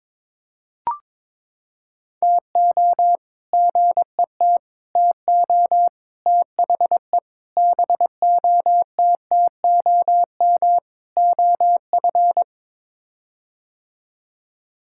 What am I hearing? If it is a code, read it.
Morse code, heard as a keyed tone, about 22 wpm, 700 Hz. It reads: TO GET TO THE BOTTOM OF
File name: practice / 22wpm / Sets of 6 Words - 22wpm